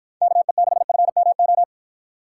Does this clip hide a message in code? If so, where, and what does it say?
Yes: Morse code from 0.2 to 1.6 s, heard as a keyed tone, about 53 wpm, 700 Hz. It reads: KE6FMO